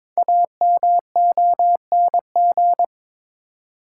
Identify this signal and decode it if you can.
Morse code, heard as a keyed tone, about 22 wpm, 700 Hz. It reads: AMONG
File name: Build A Phrase - 2-Word Phrases - 22wpm